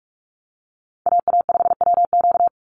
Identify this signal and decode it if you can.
Morse code, heard as a keyed tone, about 44 wpm, 700 Hz. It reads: AA5WQ
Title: USA Call Signs 45wpm